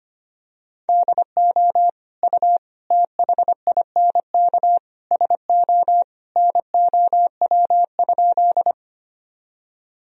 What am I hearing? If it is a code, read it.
Morse code, heard as a keyed tone, about 25 wpm, 700 Hz. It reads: DO U THINK SO NOW?
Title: Sentences from Top 200 Words and Abbreviations 25wpm